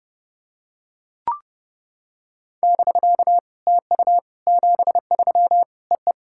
Morse code, heard as a keyed tone, about 28 wpm, 690 Hz. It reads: <BK> TU 73 EE
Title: SOTA QSO Practice 30wpm